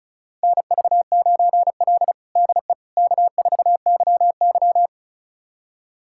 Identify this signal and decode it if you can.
Morse code, heard as a keyed tone, about 36 wpm, 700 Hz. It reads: NV9L DE K4YY
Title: USA Callsign DE Callsign - 4 Characters - 35wpm